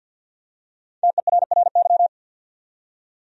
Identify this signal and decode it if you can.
Morse code, heard as a keyed tone, about 48 wpm, 700 Hz. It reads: TERRY
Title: USA Common Names - 50wpm